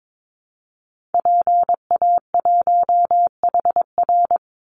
Morse code, heard as a keyed tone, about 22 wpm, 700 Hz. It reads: PA1HR
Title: Call Signs Repeated 2x - 22wpm